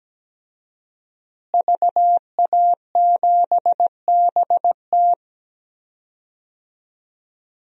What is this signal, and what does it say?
Morse code, heard as a keyed tone, about 17 wpm, 700 Hz. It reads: VA7BT
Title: DXCC-100 Call Signs - 5 Characters - Repeated 2x - 17wpm